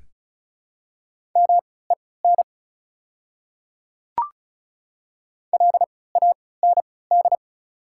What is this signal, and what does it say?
Morse code, heard as a keyed tone, about 35 wpm, spaced as for 20 wpm, 690 Hz. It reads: MEN LAND
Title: 3x Character Spacing - Top 200 Words - 35wpm